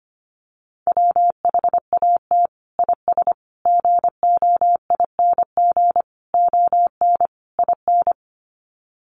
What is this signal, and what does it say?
Morse code, heard as a keyed tone, about 25 wpm, 700 Hz. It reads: WHAT IS GOING ON IN
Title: Sets of 5 Words Encore2 25wpm